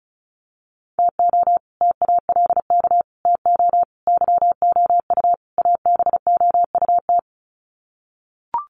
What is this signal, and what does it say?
Morse code, heard as a keyed tone, about 34 wpm, 700 Hz. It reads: TO TALK TO YOU ABOUT
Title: Sets of 5 Words Encore 35wpm